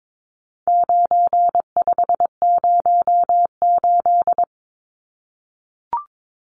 Morse code, heard as a keyed tone, about 22 wpm, 700 Hz. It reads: 9508
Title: Numbers - 4-Digit 22wpm